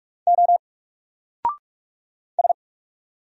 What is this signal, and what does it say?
Morse code, heard as a keyed tone, about 46 wpm, 700 Hz. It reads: O S
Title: Single Character (Letter:Number:Punctuation:Prosign) 45wpm